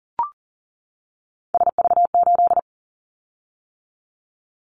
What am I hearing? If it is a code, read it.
Morse code, heard as a keyed tone, about 39 wpm, 700 Hz. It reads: SV8
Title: Callsign Prefixes for Top 50 DXCC - 40wpm